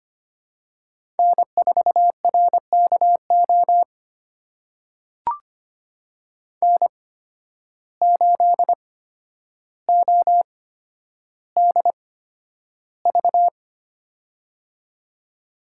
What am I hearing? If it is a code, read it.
Morse code, heard as a keyed tone, about 25 wpm, spaced as for 11 wpm, 700 Hz. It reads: N4RKO N8ODV
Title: USA Call Signs - 5 Characters - 8x Character Spacing - 25wpm